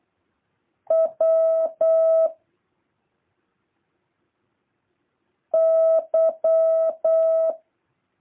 Morse code, heard as a keyed tone, about 8 wpm, 650 Hz. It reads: W Y
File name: jh8cbh / 2023cw